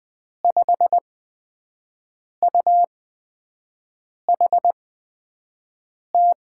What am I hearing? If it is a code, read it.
Morse code, heard as a keyed tone, about 20 wpm, spaced as for 5 wpm, 700 Hz. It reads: 5UHT
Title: USA Call Signs - 6 Characters - 8x Character Spacing - 20wpm